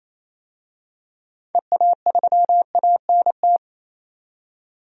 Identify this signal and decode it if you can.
Morse code, heard as a keyed tone, about 28 wpm, 700 Hz. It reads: EA3ANT
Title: DXCC-100 Call Signs - 6 Characters - 28wpm